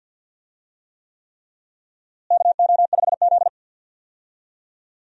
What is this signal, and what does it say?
Morse code, heard as a keyed tone, about 47 wpm, 700 Hz. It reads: KO5Z
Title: USA Call Signs - 4 Characters - Repeated 3x - 50wpm